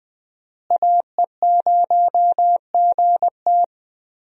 Morse code, heard as a keyed tone, about 20 wpm, 700 Hz. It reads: AE0GT
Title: USA Call Signs - 5 Characters - Repeated 2x - 20wpm